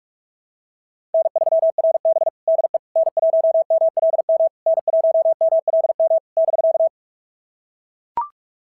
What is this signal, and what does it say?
Morse code, heard as a keyed tone, about 44 wpm, 690 Hz. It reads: N3RB DE N1MLM N1MLM <BK>
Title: SOTA QSO Practice 45wpm